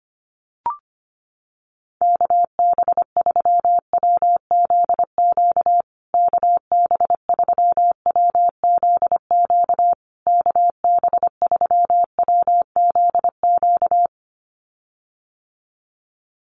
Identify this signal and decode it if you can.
Morse code, heard as a keyed tone, about 25 wpm, 700 Hz. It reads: KB3WZQ KB3WZQ KB3WZQ
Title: USA Call Signs Repeated 3x 25wpm